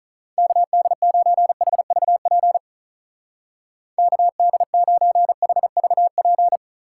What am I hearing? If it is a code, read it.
Morse code, heard as a keyed tone, about 44 wpm, 700 Hz. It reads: KD9HVP KD9HVP